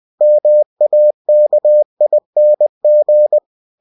Morse code, heard as a keyed tone, about 20 wpm, 600 Hz. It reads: MAKING
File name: winnie-20wpm